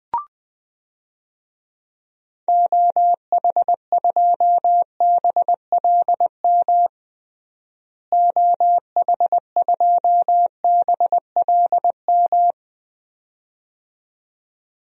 Morse code, heard as a keyed tone, about 20 wpm, 690 Hz. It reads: OH2BLM OH2BLM
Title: DXCC-100 Call Signs - 6 Characters - Repeated 2x - 20wpm